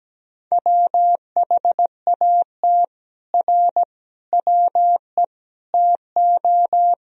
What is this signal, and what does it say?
Morse code, heard as a keyed tone, about 17 wpm, 700 Hz. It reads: WHAT R WE TO
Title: Sentences from Top 100 Words and Abbreviations 17wpm